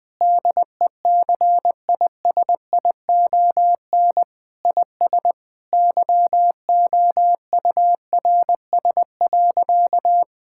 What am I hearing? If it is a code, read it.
Morse code, heard as a keyed tone, about 20 wpm, 700 Hz. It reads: DECISION IS YOURS.
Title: Sentences with 8-Word Phrases - 20wpm